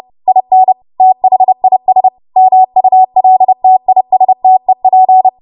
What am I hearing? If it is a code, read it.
Morse code, heard as a keyed tone, about 30 wpm, 750 Hz. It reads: IN THIS MULTISTEP
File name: Archive / 30 WPM / 30WPM